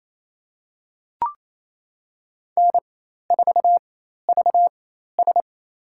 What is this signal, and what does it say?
Morse code, heard as a keyed tone, about 27 wpm, spaced as for 13 wpm, 700 Hz. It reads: N4VS
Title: USA Call Signs - 4 Characters - 4x Character Spacing - 28wpm